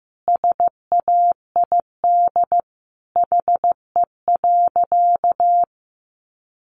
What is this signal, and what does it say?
Morse code, heard as a keyed tone, about 15 wpm, 700 Hz. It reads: SAID HE.
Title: Sentences from Top 100 Words 15wpm